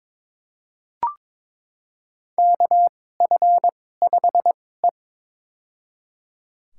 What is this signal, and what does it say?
Morse code, heard as a keyed tone, about 22 wpm, spaced as for 16 wpm, 700 Hz. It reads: KF5E
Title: USA Call Signs - 4 Characters - 2x Character Spacing - 22wpm